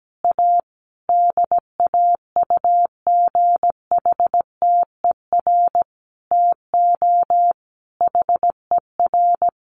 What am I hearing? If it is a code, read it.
Morse code, heard as a keyed tone, about 17 wpm, 700 Hz. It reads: A DAUGHTER TO HER
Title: Sentences from Top 1500 Words 17wpm